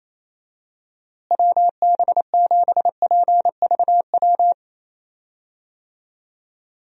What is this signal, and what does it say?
Morse code, heard as a keyed tone, about 27 wpm, 700 Hz. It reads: WB7PVW
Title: USA Call Signs - 6 Characters - Repeated 2x - 28wpm